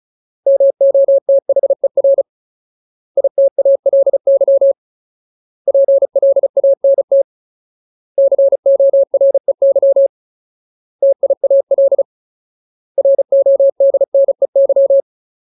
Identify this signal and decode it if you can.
Morse code, heard as a keyed tone, about 34 wpm, 550 Hz. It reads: MOTHER ITALY PLANT COREY TIAL RODNEY
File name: Ultimate Word List / 35WPM Ultimate